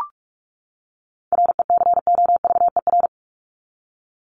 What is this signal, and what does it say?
Morse code, heard as a keyed tone, about 44 wpm, 700 Hz. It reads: RECOVER
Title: Top 1751-2000 Words 45wpm